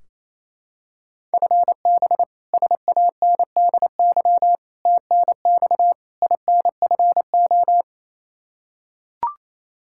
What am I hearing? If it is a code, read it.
Morse code, heard as a keyed tone, about 28 wpm, 700 Hz. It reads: FB SANDY TNX INFO
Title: Practice QSO - Exchange 3 - More Info - 28wpm